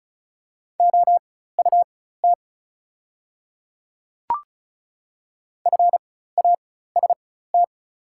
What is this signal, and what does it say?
Morse code, heard as a keyed tone, about 36 wpm, spaced as for 16 wpm, 700 Hz. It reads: OUT FAST
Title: 4x Character Spacing - Top 500 Words - 35wpm